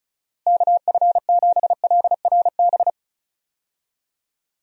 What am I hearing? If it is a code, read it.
Morse code, heard as a keyed tone, about 34 wpm, 700 Hz. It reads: KF7LRB
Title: USA Call Signs - 6 Characters - Repeated 2x - 35wpm